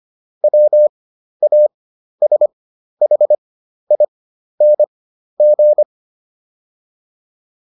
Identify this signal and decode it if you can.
Morse code, heard as a keyed tone, about 25 wpm, spaced as for 12 wpm, 600 Hz. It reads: WASHING